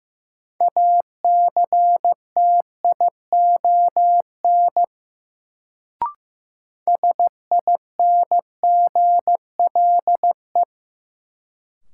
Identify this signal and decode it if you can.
Morse code, heard as a keyed tone, about 15 wpm, 700 Hz. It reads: ACTION SINGLE
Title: Sentences from Top 700 Words - New Word Drill - 15wpm